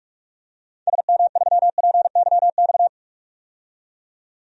Morse code, heard as a keyed tone, about 46 wpm, 700 Hz. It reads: SM3PYX
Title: DXCC-100 Call Signs - 6 Characters - 45wpm